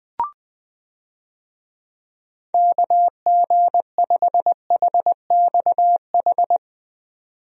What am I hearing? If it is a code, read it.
Morse code, heard as a keyed tone, about 20 wpm, 700 Hz. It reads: KG5HXH
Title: USA Call Signs - 6 Characters - 20wpm